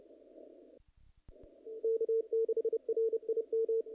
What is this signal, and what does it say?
Morse code, heard as a keyed tone, about 30 wpm, 440 Hz. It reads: K6RIM
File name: contests / naqp-cw-k5tr-op-k5pi- -1800